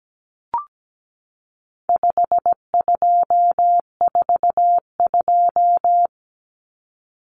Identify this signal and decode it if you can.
Morse code, heard as a keyed tone, about 17 wpm, 700 Hz. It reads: 5242
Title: Numbers - 4-Digit 17wpm